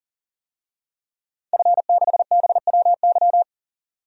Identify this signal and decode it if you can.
Morse code, heard as a keyed tone, about 40 wpm, 700 Hz. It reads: F6BWY
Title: DXCC-100 Call Signs - 5 Characters - 40wpm